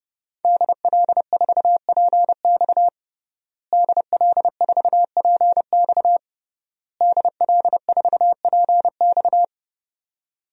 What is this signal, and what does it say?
Morse code, heard as a keyed tone, about 32 wpm, 700 Hz. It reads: DL4PX DL4PX DL4PX
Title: DXCC-100 Call Signs - 5 Characters - Repeated 3x - 30wpm